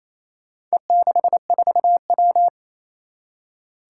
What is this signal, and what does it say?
Morse code, heard as a keyed tone, about 27 wpm, 700 Hz. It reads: E64W
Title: Four Challenging Letter-Number Combinations 28wpm